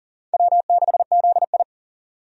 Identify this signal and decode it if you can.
Morse code, heard as a keyed tone, about 40 wpm, 700 Hz. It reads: W6ZI
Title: USA Call Signs - 4 Characters - 40wpm